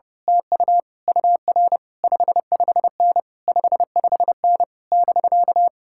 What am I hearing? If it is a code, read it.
Morse code, heard as a keyed tone, about 28 wpm, 700 Hz. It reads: TU UR 55N 55N <BK>
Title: Speed-Racing Common QSO Phrases - 30wpm